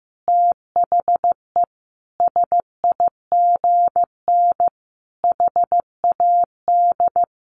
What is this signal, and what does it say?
Morse code, heard as a keyed tone, about 15 wpm, 700 Hz. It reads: THE SIGN HAD
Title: A Sound Of Thunder 15wpm